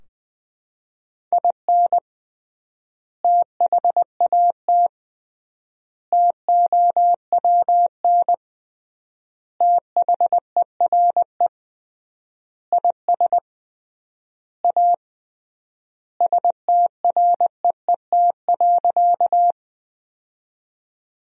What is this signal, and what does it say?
Morse code, heard as a keyed tone, about 21 wpm, 700 Hz. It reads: IN THAT TOWN THERE IS A STREET.
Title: 3x Word-Spacing - Sentences from Top 500 Words - 20wpm